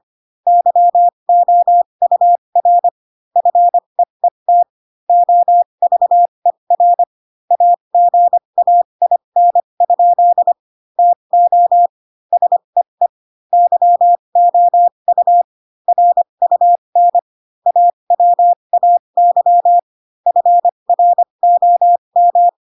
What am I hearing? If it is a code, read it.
Morse code, heard as a keyed tone, about 25 wpm, spaced as for 22 wpm, 700 Hz. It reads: YOUR FEET OVER AGAIN? TO SEE YOU RUN AWAY FROM
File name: Pinocchio / Pinocchio ch8 18 25 ik1hge